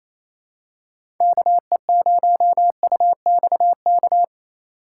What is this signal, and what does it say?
Morse code, heard as a keyed tone, about 28 wpm, 700 Hz. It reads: KE0UXK
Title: USA Call Signs - 6 Characters - 4x Character Spacing - 28wpm